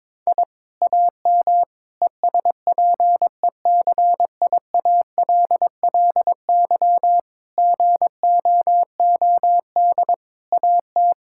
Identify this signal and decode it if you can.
Morse code, heard as a keyed tone, about 22 wpm, 700 Hz. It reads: I AM ESPECIALLY GOOD AT
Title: Sentences from Top 1000 Words - Review Part 6 - 22wpm